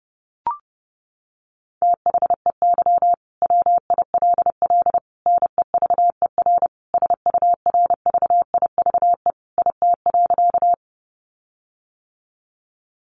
Morse code, heard as a keyed tone, about 30 wpm, 700 Hz. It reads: THEY WILL NEVER SURVIVE IT.
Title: Sentences from Top 1300 Words 30wpm